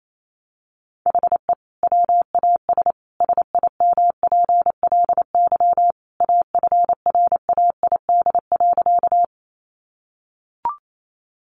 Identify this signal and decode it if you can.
Morse code, heard as a keyed tone, about 28 wpm, 700 Hz. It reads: HE WAS SIMPLY AFRAID.
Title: Sentences from Top 1500 Words 28wpm